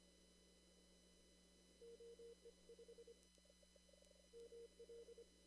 Morse code, heard as a keyed tone, about 25 wpm, spaced as for 15 wpm, 470 Hz. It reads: OE5ML
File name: CW / kt5j wpx cw-op-k5pi